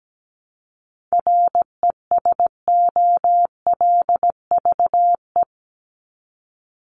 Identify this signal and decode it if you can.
Morse code, heard as a keyed tone, about 17 wpm, 700 Hz. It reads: RESOLVE
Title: Top 1751-2000 Words 17wpm